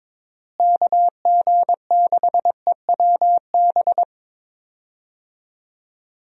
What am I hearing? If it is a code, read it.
Morse code, heard as a keyed tone, about 22 wpm, 700 Hz. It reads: KG6EWB
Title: USA Call Signs - 6 Characters - Repeated 3x - 22wpm